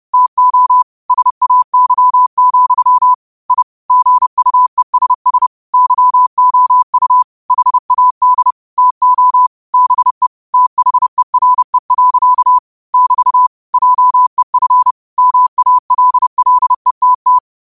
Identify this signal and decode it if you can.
Morse code, heard as a keyed tone, about 30 wpm, 1000 Hz. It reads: TO SAY, I GUESS YOU HAD TO BE THERE. <BT> JEF MALLETT
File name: qotdfiles / 30wpm